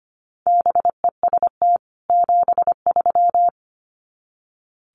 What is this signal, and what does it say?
Morse code, heard as a keyed tone, about 25 wpm, 700 Hz. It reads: BEST 73
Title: Common QSO Phrases 25wpm